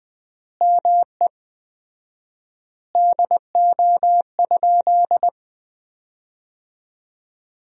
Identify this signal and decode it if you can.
Morse code, heard as a keyed tone, about 21 wpm, 700 Hz. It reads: ME DO?
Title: 4x Word-Spacing - Sentences from Top 200 Words - 20wpm